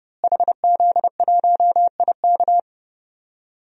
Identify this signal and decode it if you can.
Morse code, heard as a keyed tone, about 30 wpm, 700 Hz. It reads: HZ1IK
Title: DXCC-100 Call Signs - 5 Characters - Repeated 3x - 30wpm